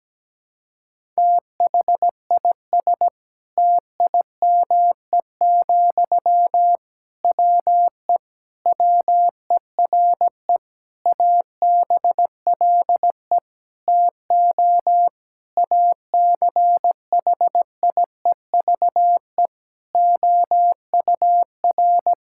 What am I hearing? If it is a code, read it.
Morse code, heard as a keyed tone, about 17 wpm, 700 Hz. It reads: THIS TIME, WE WERE ABLE TO ACHIEVE OUR
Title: Sentences from Top 1000 Words - Review Part 6 - 17wpm